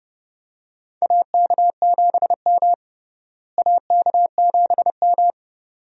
Morse code, heard as a keyed tone, about 30 wpm, 700 Hz. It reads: AK7M AK7M
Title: USA Call Signs - 4 Characters - Repeated 2x - 30wpm